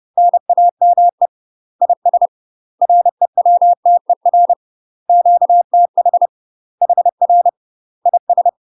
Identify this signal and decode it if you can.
Morse code, heard as a keyed tone, about 30 wpm, 700 Hz. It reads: NAME IS REWTER QTH HR IS
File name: UK MASTER QSOs TRAINING / 30WPM / UK MASTER QSOs 0014